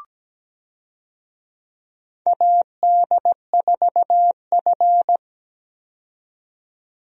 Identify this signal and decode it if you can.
Morse code, heard as a keyed tone, about 17 wpm, 700 Hz. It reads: AD4F
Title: USA Call Signs - 4 Characters - 17wpm